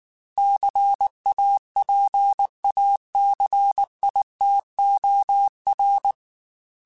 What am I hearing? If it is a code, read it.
Morse code, heard as a keyed tone, about 19 wpm, 800 Hz. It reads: CAPACITOR